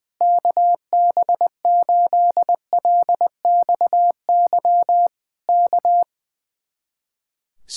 Morse code, heard as a keyed tone, about 20 wpm, 700 Hz. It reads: KB8LXY K
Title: Speed-Racing Common QSO Phrases - 20wpm